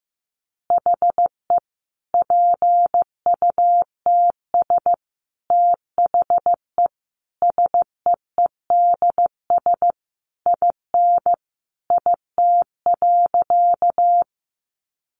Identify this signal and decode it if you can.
Morse code, heard as a keyed tone, about 15 wpm, 700 Hz. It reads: HE PUTS THE SEEDS IN IT.
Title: Easy Phrases 15wpm